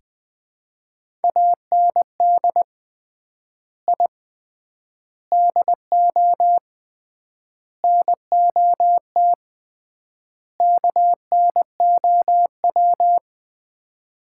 Morse code, heard as a keyed tone, about 20 wpm, 690 Hz. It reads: AND I DO NOT KNOW
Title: 3x Word-Spacing - Sets of 5 Words - 20wpm